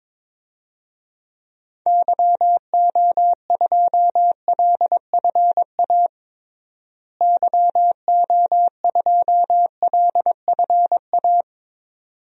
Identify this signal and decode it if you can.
Morse code, heard as a keyed tone, about 22 wpm, 700 Hz. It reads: YO2LFA YO2LFA
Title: DXCC-100 Call Signs - 6 Characters - Repeated 2x - 22wpm